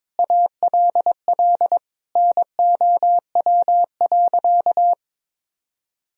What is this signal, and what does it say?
Morse code, heard as a keyed tone, about 21 wpm, 700 Hz. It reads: ALL NOW.
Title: Sentences from Top 1000 Words - Review Part 1 - 22wpm